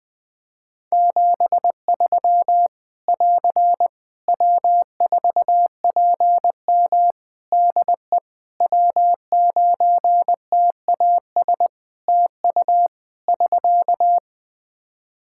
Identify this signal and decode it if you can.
Morse code, heard as a keyed tone, about 21 wpm, 700 Hz. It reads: 73 <AR> W4PM DE W9TAS TU <SK>
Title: K1USN SST QSO Practice 20wpm